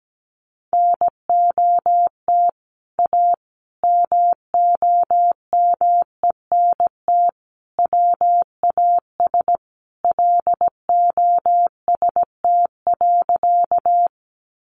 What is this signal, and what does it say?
Morse code, heard as a keyed tone, about 17 wpm, 700 Hz. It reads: NOT A MOMENT WAS LOST.
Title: Sentences from Top 1400 Words 17wpm